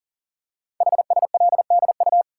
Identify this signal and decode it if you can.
Morse code, heard as a keyed tone, about 42 wpm, 700 Hz. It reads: HSLDU
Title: Code Groups - 5 Letter:Numbers - 40wpm